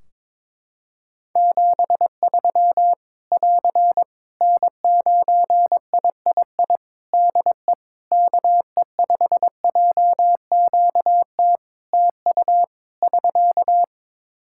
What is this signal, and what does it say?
Morse code, heard as a keyed tone, about 22 wpm, 700 Hz. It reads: 73 <AR> N9III DE KE5JQT TU <SK>
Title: Practice QSO - Exchange 4 - Ending - 22wpm